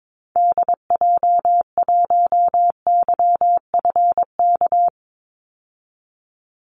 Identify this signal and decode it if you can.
Morse code, heard as a keyed tone, about 22 wpm, 700 Hz. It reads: DJ1YFK
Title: Call Signs Repeated 2x - 22wpm